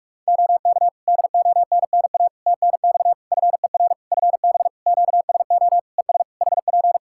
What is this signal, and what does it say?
Morse code, heard as a keyed tone, about 42 wpm, 700 Hz. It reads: OK DONNA TNX FER FB QSO ES HP
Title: Practice QSO - Exchange 4 - Ending - 45wpm